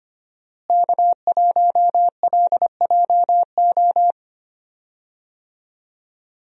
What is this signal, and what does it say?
Morse code, heard as a keyed tone, about 25 wpm, 700 Hz. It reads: K1LJO
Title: USA Call Signs - 5 Characters - Repeated 2x - 25wpm